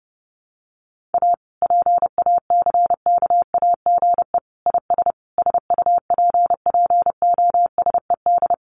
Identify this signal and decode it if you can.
Morse code, heard as a keyed tone, about 30 wpm, 700 Hz. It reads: A PACKAGE IS SUPPOSED
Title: Easy Phrases 30wpm